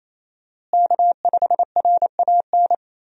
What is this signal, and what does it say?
Morse code, heard as a keyed tone, about 28 wpm, 700 Hz. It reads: K5RAN
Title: USA Call Signs - 5 Characters - Repeated 3x - 28wpm